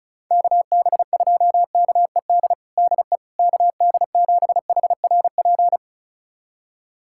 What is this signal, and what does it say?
Morse code, heard as a keyed tone, about 36 wpm, 700 Hz. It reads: KB2KED DE KD7HRP
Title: USA Callsign DE Callsign 35wpm